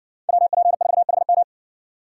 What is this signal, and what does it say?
Morse code, heard as a keyed tone, about 58 wpm, 700 Hz. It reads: WP4HM